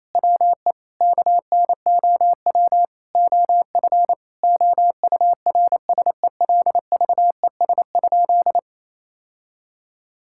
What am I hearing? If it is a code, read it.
Morse code, heard as a keyed tone, about 27 wpm, 700 Hz. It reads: WE KNOW OF OURSELVES?
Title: Sentences from Top 1600 Words 28wpm